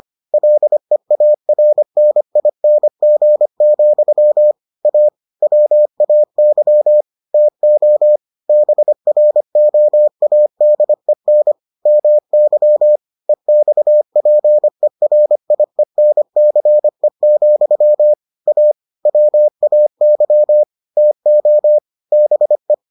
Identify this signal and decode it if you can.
Morse code, heard as a keyed tone, about 25 wpm, 600 Hz. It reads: LEARNING, A WAY TO BROADEN MY EXPERIENCE, A WAY TO BE